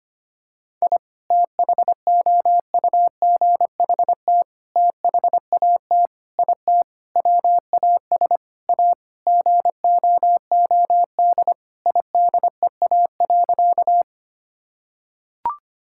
Morse code, heard as a keyed tone, about 25 wpm, 700 Hz. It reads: I THOUGHT THAT IT WAS A GOOD IDEA.
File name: practice / 25wpm / Sentences with 4-Word Phrases - 25wpm